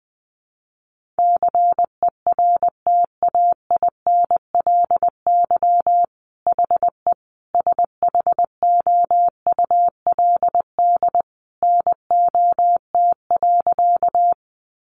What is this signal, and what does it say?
Morse code, heard as a keyed tone, about 20 wpm, 700 Hz. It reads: CERTAINLY HE SHOULD NOT.
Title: Sentences from Top 900 Words 20wpm